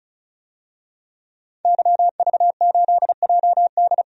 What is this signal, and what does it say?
Morse code, heard as a keyed tone, about 34 wpm, 700 Hz. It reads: YV8JD
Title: DXCC-100 Call Signs - 5 Characters - 35wpm